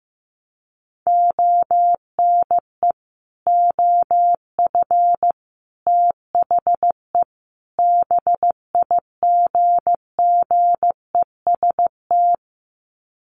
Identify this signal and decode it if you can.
Morse code, heard as a keyed tone, about 15 wpm, 700 Hz. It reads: ONE OF THE BIGGEST
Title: Sets of 4 Words 15wpm